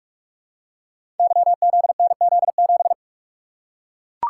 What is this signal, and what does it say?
Morse code, heard as a keyed tone, about 44 wpm, 690 Hz. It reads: YZNZ7
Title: Code Groups - 5 Letter:Numbers - 45wpm